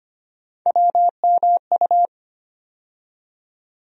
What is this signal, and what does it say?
Morse code, heard as a keyed tone, about 25 wpm, 700 Hz. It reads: WMU
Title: USA Call Sign Postfixes - 3 Letters 25wpm